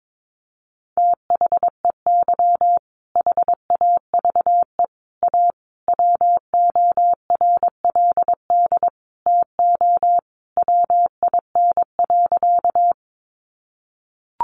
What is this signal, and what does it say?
Morse code, heard as a keyed tone, about 22 wpm, 700 Hz. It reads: THEY HAVE A WORLD TO WIN.
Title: Sentences from Top 1500 Words 22wpm